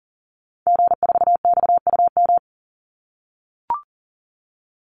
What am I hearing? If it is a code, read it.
Morse code, heard as a keyed tone, about 39 wpm, 700 Hz. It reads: G4XUM
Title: Call Signs - 40wpm